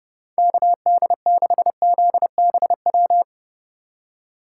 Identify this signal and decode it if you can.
Morse code, heard as a keyed tone, about 29 wpm, 700 Hz. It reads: KD6ZBW
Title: USA Call Signs - 2x Character Spacing - 30wpm